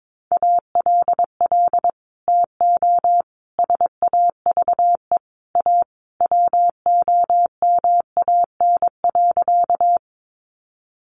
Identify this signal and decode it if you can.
Morse code, heard as a keyed tone, about 22 wpm, 700 Hz. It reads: ALL TO SAVE A WOMAN.
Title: Sentences from Top 1100 Words 22wpm